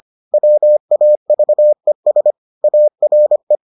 Morse code, heard as a keyed tone, about 25 wpm, 600 Hz. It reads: WAVES ARE